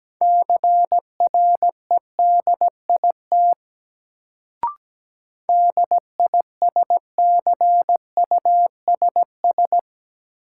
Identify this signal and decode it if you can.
Morse code, heard as a keyed tone, about 17 wpm, 700 Hz. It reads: CREDIT DISCUSS
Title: Sentences from Top 1000 Words - New Word Drill - 17wpm